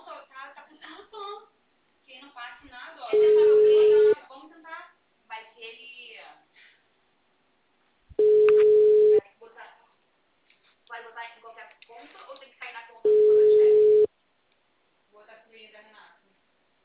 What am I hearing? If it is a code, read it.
Ringback tone (Europe, 425 Hz), with speech.
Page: Forca - RJ